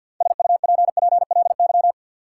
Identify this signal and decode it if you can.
Morse code, heard as a keyed tone, about 49 wpm, 700 Hz. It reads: SUPPLY
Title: Sentences from Top 1200 Words - New Word Drill Alt - 50wpm